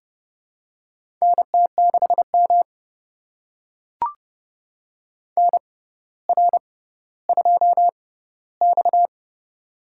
Morse code, heard as a keyed tone, about 32 wpm, spaced as for 16 wpm, 700 Hz. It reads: NT6M NR2X
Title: USA Call Signs - 4 Characters - 6x Character Spacing - 30wpm